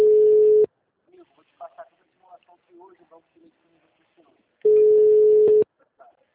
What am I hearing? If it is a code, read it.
Ringback tone (Europe, 425 Hz), with speech.